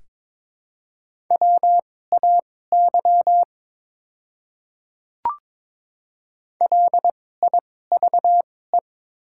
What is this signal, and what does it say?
Morse code, heard as a keyed tone, about 21 wpm, spaced as for 16 wpm, 700 Hz. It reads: WAY LIVE
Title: 2x Character Spacing - Top 200 Words - 22wpm